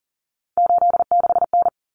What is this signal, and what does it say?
Morse code, heard as a keyed tone, about 40 wpm, 700 Hz. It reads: 86N
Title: Common Cut Numbers - 3-Digit 40wpm